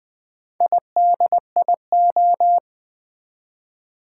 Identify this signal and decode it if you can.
Morse code, heard as a keyed tone, about 20 wpm, 700 Hz. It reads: IDIO
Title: Build A Word - 1 Letter At A Time - 5-Letter Words - 20wpm